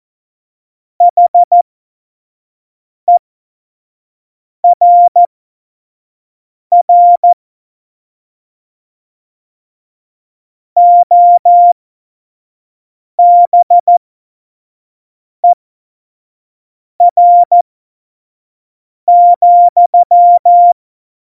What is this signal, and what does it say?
Morse code, heard as a keyed tone, about 14 wpm, spaced as for 5 wpm, 700 Hz. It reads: HERR OBER,
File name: Morsekurs / Witze / Witz 19. November 2019, 25 F70